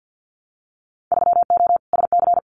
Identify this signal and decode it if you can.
Morse code, heard as a keyed tone, about 48 wpm, 700 Hz. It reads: 3O SC
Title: Winter Field Day Exchanges 50wpm